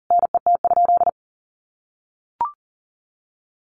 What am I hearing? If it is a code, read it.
Morse code, heard as a keyed tone, about 40 wpm, 700 Hz. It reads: NET?
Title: Sentences from Top 1400 Words 40wpm